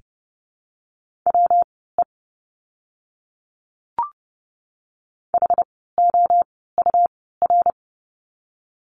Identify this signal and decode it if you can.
Morse code, heard as a keyed tone, about 30 wpm, spaced as for 17 wpm, 700 Hz. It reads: WE HOUR
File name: course / 30:10wpm / 115 Lesson 19f - Letter w - Words 30wpm